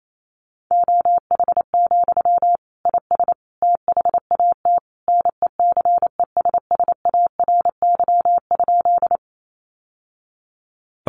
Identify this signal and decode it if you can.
Morse code, heard as a keyed tone, about 28 wpm, 700 Hz. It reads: OH, IS THAT NECESSARY?
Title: Sentences from Top 800 Words 28wpm